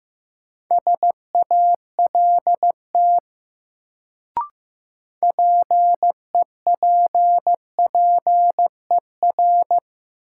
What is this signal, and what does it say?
Morse code, heard as a keyed tone, about 15 wpm, 700 Hz. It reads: SALT PEPPER
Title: Sets of 5 Words - New Word Drill - 15wpm